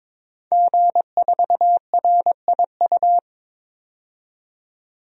Morse code, heard as a keyed tone, about 22 wpm, 700 Hz. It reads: G4RIU
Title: DXCC-100 Call Signs - 5 Characters - 22wpm